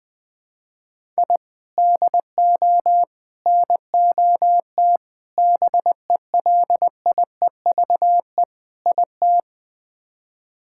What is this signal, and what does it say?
Morse code, heard as a keyed tone, about 20 wpm, 700 Hz. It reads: I DO NOT BELIEVE IT
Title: Build A Phrase - 5-Word Phrases - 20wpm